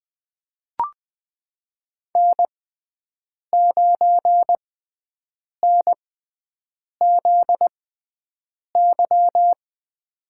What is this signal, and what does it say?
Morse code, heard as a keyed tone, about 20 wpm, spaced as for 7 wpm, 700 Hz. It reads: N9NZY
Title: USA Call Signs - 5 Characters - 6x Character Spacing - 20wpm